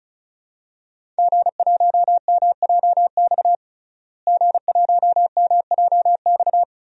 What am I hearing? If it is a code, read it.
Morse code, heard as a keyed tone, about 36 wpm, 700 Hz. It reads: G1MJX G1MJX
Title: DXCC-100 Call Signs - 5 Characters - Repeated 3x - 35wpm